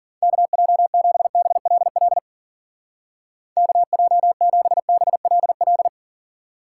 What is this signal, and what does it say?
Morse code, heard as a keyed tone, about 49 wpm, 700 Hz. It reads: KJ7BLL KJ7BLL
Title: Speed Racing - USA 2x3 Call Signs 40wpm